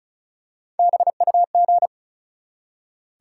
Morse code, heard as a keyed tone, about 36 wpm, 700 Hz. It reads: BUG
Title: Three Letter Words - 35wpm